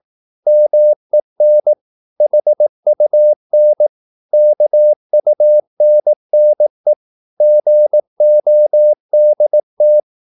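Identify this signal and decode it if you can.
Morse code, heard as a keyed tone, about 18 wpm, 600 Hz. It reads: MEN HUN KUNNE GODT
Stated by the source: Slow (QRS) Morse practice at 18 wpm, CW med Gnister.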